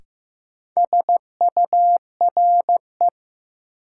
Morse code, heard as a keyed tone, about 16 wpm, 700 Hz. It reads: SURE
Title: Sentences from Top 300 Words - New Word Drill - 15wpm